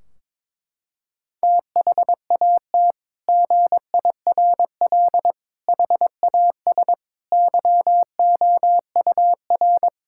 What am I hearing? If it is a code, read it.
Morse code, heard as a keyed tone, about 22 wpm, 700 Hz. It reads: THAT GIRL HAS YOUR
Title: Sentences from Top 1600 Words 22wpm